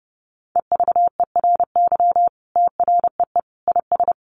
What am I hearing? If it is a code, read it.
Morse code, heard as a keyed tone, about 30 wpm, 700 Hz. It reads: EVERY TREE IS
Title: Sentences from Top 1300 Words 30wpm